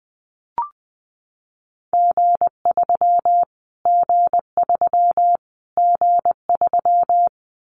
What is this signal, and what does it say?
Morse code, heard as a keyed tone, about 20 wpm, 700 Hz. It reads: G3 G3 G3
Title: Callsign Prefixes for Top 25 DXCC - Repeated 3x - 20wpm